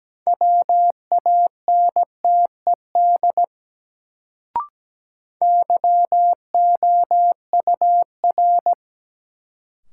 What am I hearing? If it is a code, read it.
Morse code, heard as a keyed tone, about 17 wpm, 700 Hz. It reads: WANTED YOUR
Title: Sets of 2 Words - New Word Drill - 17wpm